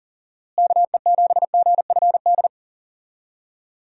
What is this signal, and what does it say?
Morse code, heard as a keyed tone, about 40 wpm, 700 Hz. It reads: KE7GFD
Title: USA Call Signs - 6 Characters - Repeated 3x - 40wpm